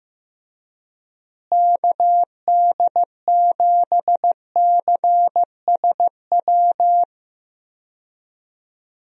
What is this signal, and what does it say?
Morse code, heard as a keyed tone, about 16 wpm, 700 Hz. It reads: KD7CSW
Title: USA Call Signs - 6 Characters - Repeated 3x - 15wpm